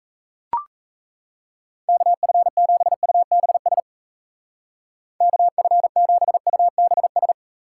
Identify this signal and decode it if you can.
Morse code, heard as a keyed tone, about 37 wpm, 700 Hz. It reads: KF7UBS KF7UBS